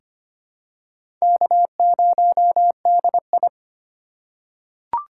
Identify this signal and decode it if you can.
Morse code, heard as a keyed tone, about 25 wpm, 700 Hz. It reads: K0DI
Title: USA Call Signs - 4 Characters - 3x Character Spacing - 25wpm